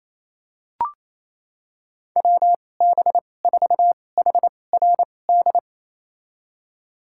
Morse code, heard as a keyed tone, about 28 wpm, spaced as for 20 wpm, 700 Hz. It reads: WB4HRD
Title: USA Call Signs - 2x Character Spacing - 28wpm